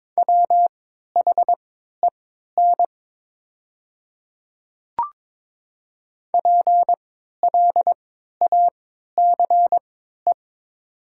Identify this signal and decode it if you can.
Morse code, heard as a keyed tone, about 22 wpm, spaced as for 12 wpm, 700 Hz. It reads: WHEN PLACE
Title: 3x Character Spacing - Top 300 Words - 22wpm